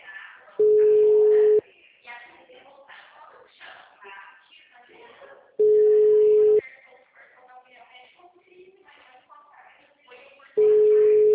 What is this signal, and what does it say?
Ringback tone (Europe, 425 Hz), with speech.